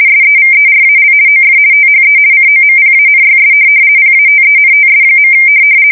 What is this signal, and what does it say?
RTTY, about 50 baud. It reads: SEVERAL OF THE COMPANYS PATENTS.
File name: fsk170-50B/teletype-corp